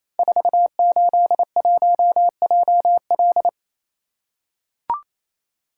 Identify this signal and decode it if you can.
Morse code, heard as a keyed tone, about 27 wpm, 700 Hz. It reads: 481JL
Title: Code Groups - 5 Letter:Numbers - 28wpm